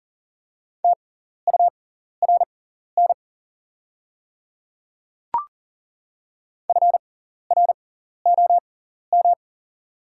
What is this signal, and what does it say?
Morse code, heard as a keyed tone, about 40 wpm, spaced as for 14 wpm, 690 Hz. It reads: TURN FROM
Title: 6x Character Spacing - Top 300 Words - 40wpm